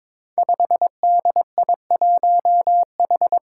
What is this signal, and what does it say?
Morse code, heard as a keyed tone, about 22 wpm, 700 Hz. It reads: 5DI1H
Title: Five Challenging Letter-Number Combinations 22wpm